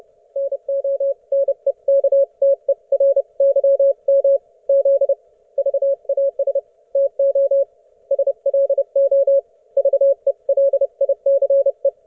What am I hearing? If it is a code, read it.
Morse code, heard as a keyed tone, about 30 wpm, 550 Hz. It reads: NONEKTERYM Z VAS TO SLO VELICE
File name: Archiv / CUC181 / ok1fhi cul181